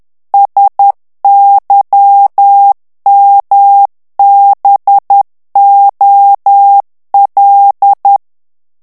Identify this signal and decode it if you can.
Morse code, heard as a keyed tone, about 10 wpm, 800 Hz. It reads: SYMBOL
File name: ANGLAIS texte clair 3 vitesse 10 mots minute